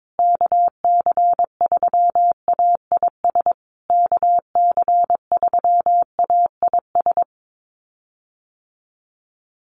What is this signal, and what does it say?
Morse code, heard as a keyed tone, about 22 wpm, 700 Hz. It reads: KC3AIS KC3AIS
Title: USA Call Signs Repeated 3x 22wpm